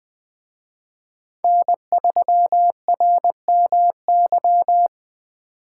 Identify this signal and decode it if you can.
Morse code, heard as a keyed tone, about 23 wpm, 700 Hz. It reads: N3RMY
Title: Speed Racing - USA 1x3 Call Signs 20wpm